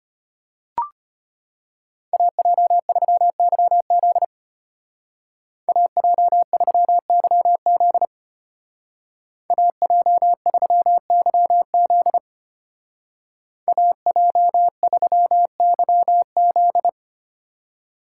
Morse code, heard as a keyed tone, about 33 wpm, 700 Hz. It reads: AJ3YZ AJ3YZ AJ3YZ AJ3YZ